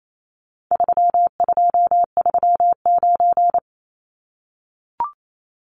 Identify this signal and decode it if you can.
Morse code, heard as a keyed tone, about 28 wpm, 700 Hz. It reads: 3239
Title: Numbers - 4-Digit 28wpm